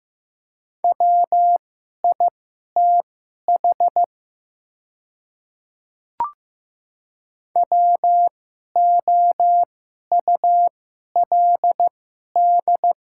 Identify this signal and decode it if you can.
Morse code, heard as a keyed tone, about 15 wpm, spaced as for 11 wpm, 700 Hz. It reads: WITH WOULD
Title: 2x Character Spacing - Top 100 Words - 15wpm